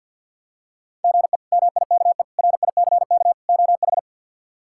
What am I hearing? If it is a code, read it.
Morse code, heard as a keyed tone, about 47 wpm, 690 Hz. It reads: GE MIKE RICK OH
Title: K1USN SST QSO Practice - 50wpm